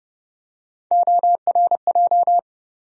Morse code, heard as a keyed tone, about 30 wpm, 700 Hz. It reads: ORJ
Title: USA Call Sign Postfixes - 3 Letters 30wpm